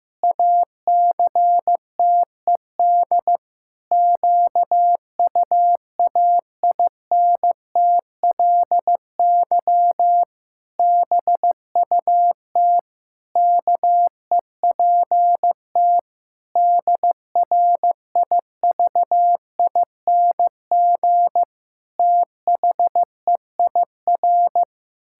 Morse code, heard as a keyed tone, about 15 wpm, 700 Hz. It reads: ACTED QUAINTLY BUT KEPT DRIVING THEIR
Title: Pangram Challenge v2 - 15wpm